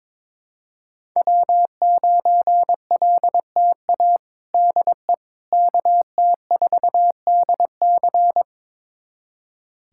Morse code, heard as a keyed tone, about 22 wpm, 700 Hz. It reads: W9LTA DE KT4DC
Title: USA Callsign DE Callsign - 5 Characters - 22wpm